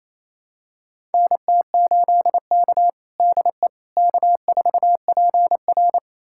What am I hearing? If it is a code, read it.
Morse code, heard as a keyed tone, about 29 wpm, 690 Hz. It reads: NT8K DE K4PR
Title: USA Callsign DE Callsign - 4 Characters - 28wpm